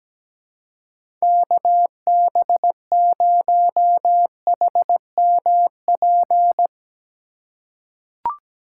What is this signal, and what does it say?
Morse code, heard as a keyed tone, about 17 wpm, 700 Hz. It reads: KB0HMP
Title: USA Call Signs - 6 Characters - 4x Character Spacing - 17wpm